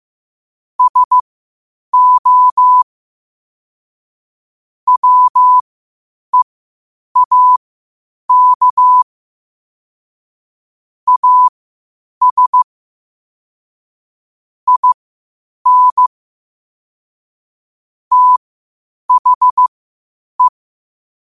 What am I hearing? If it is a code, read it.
Morse code, heard as a keyed tone, about 15 wpm, spaced as for 8 wpm, 1000 Hz. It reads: SO WEAK AS IN THE
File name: qotdfiles / 8wpm